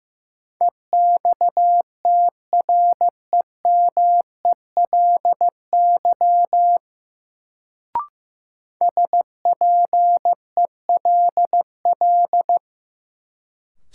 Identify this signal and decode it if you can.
Morse code, heard as a keyed tone, about 15 wpm, 700 Hz. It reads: EXTREMELY SPELL
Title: Sentences from Top 1200 Words - New Word Drill Alt - 15wpm